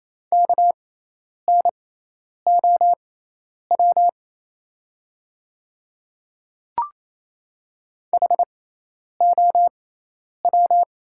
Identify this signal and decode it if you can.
Morse code, heard as a keyed tone, about 28 wpm, spaced as for 10 wpm, 700 Hz. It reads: KNOW HOW
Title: 6x Character Spacing - Top 100 Words - 28wpm